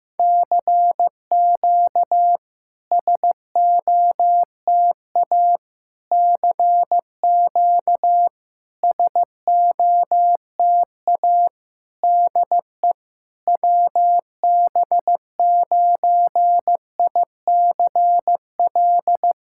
Morse code, heard as a keyed tone, about 15 wpm, 700 Hz. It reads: CQ SOTA CQ SOTA DE WB9ICL
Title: SOTA QSO Practice 15wpm